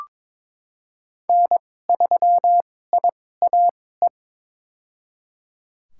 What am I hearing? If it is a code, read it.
Morse code, heard as a keyed tone, about 22 wpm, spaced as for 16 wpm, 700 Hz. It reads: N3IAE
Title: USA Call Signs - 2x Character Spacing - 22wpm